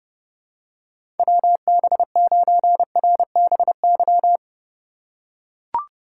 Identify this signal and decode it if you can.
Morse code, heard as a keyed tone, about 30 wpm, 700 Hz. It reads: WB9RBY
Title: USA Call Signs - 6 Characters - 2x Character Spacing - 30wpm